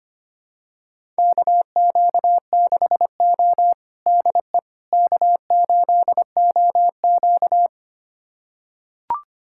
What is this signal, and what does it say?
Morse code, heard as a keyed tone, about 25 wpm, 700 Hz. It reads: KQ6O DE K8OQ
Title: USA Callsign DE Callsign - 4 Characters - 25wpm